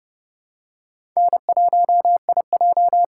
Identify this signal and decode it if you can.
Morse code, heard as a keyed tone, about 30 wpm, 700 Hz. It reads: N1IJ
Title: USA Call Signs - 4 Characters - 3x Character Spacing - 30wpm